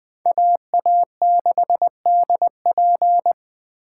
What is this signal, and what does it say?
Morse code, heard as a keyed tone, about 20 wpm, 700 Hz. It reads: AA6DP